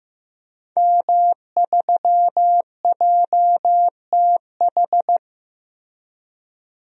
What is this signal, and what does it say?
Morse code, heard as a keyed tone, about 15 wpm, 700 Hz. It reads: M3JTH
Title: DXCC-100 Call Signs - 5 Characters - 15wpm